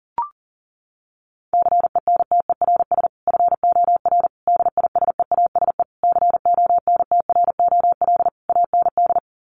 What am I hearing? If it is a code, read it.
Morse code, heard as a keyed tone, about 40 wpm, 700 Hz. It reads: CENTERS FOR DISEASE CONTROL AND
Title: Sets of 5 Words Encore2 40wpm